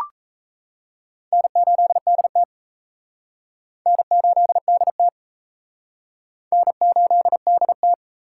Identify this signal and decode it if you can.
Morse code, heard as a keyed tone, about 35 wpm, 700 Hz. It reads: N8DT N8DT N8DT
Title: Speed Racing - USA 1x2 Call Signs 28wpm